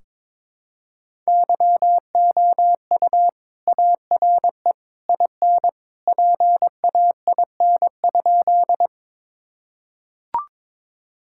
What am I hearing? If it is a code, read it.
Morse code, heard as a keyed tone, about 22 wpm, 700 Hz. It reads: YOU ARE IN PAIN?
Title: Sentences from Top 1600 Words 22wpm